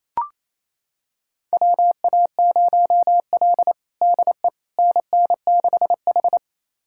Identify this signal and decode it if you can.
Morse code, heard as a keyed tone, about 27 wpm, 700 Hz. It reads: WA0L DE NN6H
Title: USA Callsign DE Callsign - 4 Characters - 28wpm